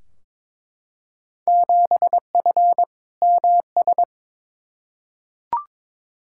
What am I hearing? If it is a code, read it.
Morse code, heard as a keyed tone, about 22 wpm, 700 Hz. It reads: 7F MS
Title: Field Day Exchanges Encore - 22wpm